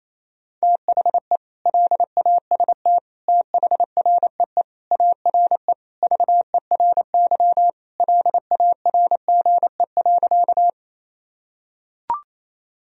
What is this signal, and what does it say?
Morse code, heard as a keyed tone, about 28 wpm, 700 Hz. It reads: THE LAST THREE ARE VERY LARGE.
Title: Sentences from Top 1000 Words - Review Part 1 - 28wpm